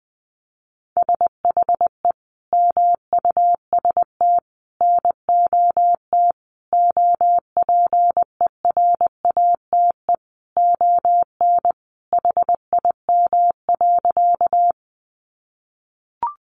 Morse code, heard as a keyed tone, about 20 wpm, 700 Hz. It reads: SHE MUST NOT OPERATE ON HIM.
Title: Sentences from Top 1500 Words 20wpm